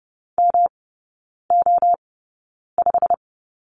Morse code, heard as a keyed tone, about 30 wpm, spaced as for 9 wpm, 700 Hz. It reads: MO5
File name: Lesson 18e - Letter m - Three Characters 30wpm